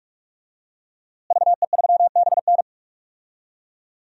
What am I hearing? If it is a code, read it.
Morse code, heard as a keyed tone, about 47 wpm, 700 Hz. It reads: VE3BN
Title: DXCC-100 Call Signs - 5 Characters - Repeated 2x - 45wpm